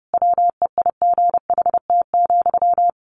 Morse code, heard as a keyed tone, about 30 wpm, 700 Hz. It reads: WEIGHT,
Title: A Sound Of Thunder 30wpm